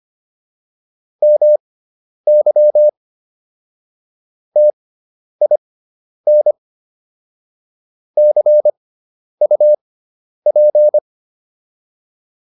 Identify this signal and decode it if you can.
Morse code, heard as a keyed tone, about 25 wpm, spaced as for 10 wpm, 600 Hz. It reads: MY TIN CUP